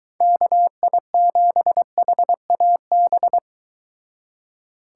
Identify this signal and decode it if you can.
Morse code, heard as a keyed tone, about 22 wpm, 700 Hz. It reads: KI7HAB